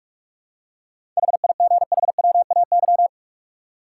Morse code, heard as a keyed tone, about 46 wpm, 700 Hz. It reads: HIGHWAY
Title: Seven Letter Words - 45wpm